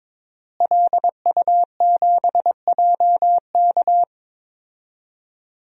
Morse code, heard as a keyed tone, about 22 wpm, 700 Hz. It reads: LU7JK
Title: DXCC-100 Call Signs - 5 Characters - Repeated 2x - 22wpm